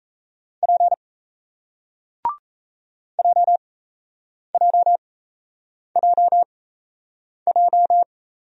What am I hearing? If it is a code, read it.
Morse code, heard as a keyed tone, about 39 wpm, spaced as for 8 wpm, 700 Hz. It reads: P JJJJ